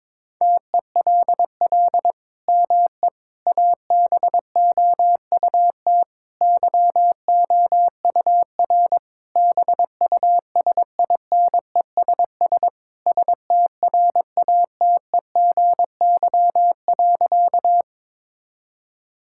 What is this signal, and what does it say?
Morse code, heard as a keyed tone, about 22 wpm, 700 Hz. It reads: TELL ME ABOUT YOUR BUSINESS STRATEGY.
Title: Sentences from Top 1000 Words - Review Part 5 - 22wpm